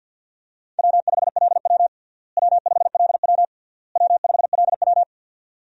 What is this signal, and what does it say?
Morse code, heard as a keyed tone, about 50 wpm, 700 Hz. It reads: W5LW W5LW W5LW
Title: USA Call Signs - 4 Characters - Repeated 3x - 50wpm